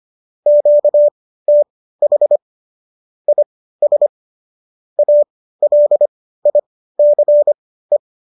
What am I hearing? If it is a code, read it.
Morse code, heard as a keyed tone, about 25 wpm, spaced as for 15 wpm, 590 Hz. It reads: QTH IS ALICE
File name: qso204 15